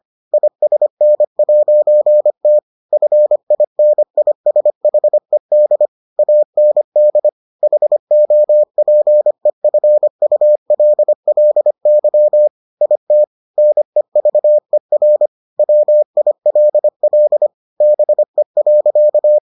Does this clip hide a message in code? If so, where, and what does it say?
Yes: Morse code from 0.3 to 19.4 s, heard as a keyed tone, about 25 wpm, 600 Hz. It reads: ISN'T FINISHED AND HOPEFULLY IT NEVER WILL BE.